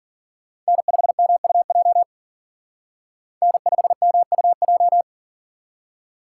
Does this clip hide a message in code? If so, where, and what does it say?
Yes: Morse code from 0.7 to 5.0 s, heard as a keyed tone, about 51 wpm, 700 Hz. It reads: N5MUJ N5MUJ